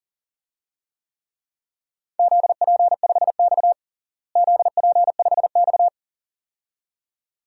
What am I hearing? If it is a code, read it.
Morse code, heard as a keyed tone, about 37 wpm, 700 Hz. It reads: ZP5X ZP5X
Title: DXCC-100 Call Signs - 4 Characters - Repeated 2x - 40wpm